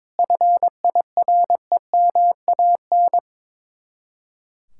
Morse code, heard as a keyed tone, about 22 wpm, 700 Hz. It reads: FIREMAN
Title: Speed-Racing Common QSO Phrases - 22wpm